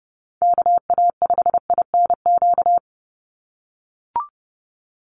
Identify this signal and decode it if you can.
Morse code, heard as a keyed tone, about 30 wpm, 700 Hz. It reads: KA5INQ
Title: USA Call Signs 30wpm